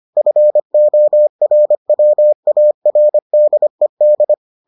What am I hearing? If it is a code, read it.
Morse code, heard as a keyed tone, about 25 wpm, 600 Hz. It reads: FORWARDED